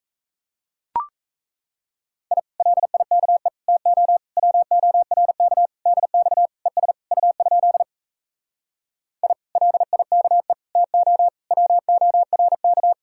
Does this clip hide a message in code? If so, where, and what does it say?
Yes: Morse code from 2.3 to 12.9 s, heard as a keyed tone, about 37 wpm, 700 Hz. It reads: I LIKE TO WORK DX ES U? I LIKE TO WORK